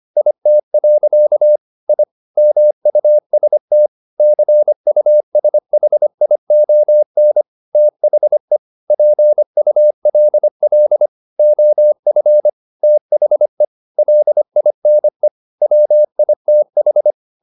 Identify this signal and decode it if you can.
Morse code, heard as a keyed tone, about 25 wpm, 600 Hz. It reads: IT. I MUST CUSHION THE PULL OF THE LINE WITH